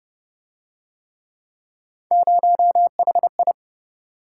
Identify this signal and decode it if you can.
Morse code, heard as a keyed tone, about 29 wpm, 700 Hz. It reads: 0HI